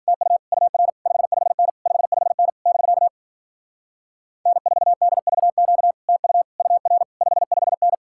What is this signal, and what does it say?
Morse code, heard as a keyed tone, about 47 wpm, 700 Hz. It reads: TU UR 55N 55N <BK> N4DVQ TU UR 55N
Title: Speed-Racing Common QSO Phrases Encore - 40wpm